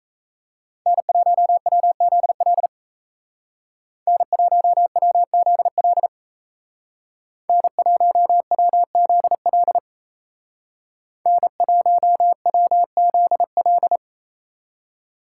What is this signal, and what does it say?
Morse code, heard as a keyed tone, about 37 wpm, 700 Hz. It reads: N1WZL N1WZL N1WZL N1WZL